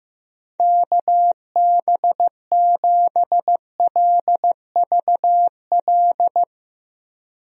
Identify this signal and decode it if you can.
Morse code, heard as a keyed tone, about 15 wpm, 700 Hz. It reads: KB7LVL
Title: USA Call Signs - 6 Characters - Repeated 3x - 15wpm